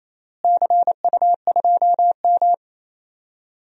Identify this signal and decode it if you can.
Morse code, heard as a keyed tone, about 28 wpm, 700 Hz. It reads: CU2M
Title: DXCC-100 Call Signs - 4 Characters - Repeated 2x - 28wpm